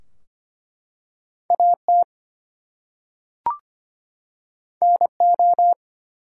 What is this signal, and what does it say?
Morse code, heard as a keyed tone, about 25 wpm, 700 Hz. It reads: AT NO
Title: Two Letter Words - 25wpm